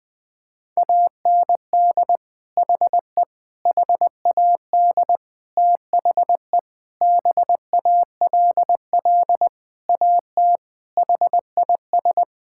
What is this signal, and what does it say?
Morse code, heard as a keyed tone, about 20 wpm, 700 Hz. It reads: AND HE HAD THE BALL AT HIS
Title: Sentences from Top 1000 Words - Review Part 6 - 20wpm